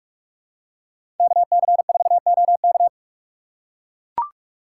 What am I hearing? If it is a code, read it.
Morse code, heard as a keyed tone, about 42 wpm, 700 Hz. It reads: KC4OK
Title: USA Call Signs - 5 Characters - 4x Character Spacing - 45wpm